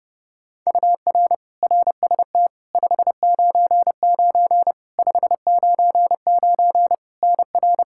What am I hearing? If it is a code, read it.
Morse code, heard as a keyed tone, about 30 wpm, 700 Hz. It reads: UR RST 599 599 NR
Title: Practice QSO - Exchange 1 - Beginning - 30wpm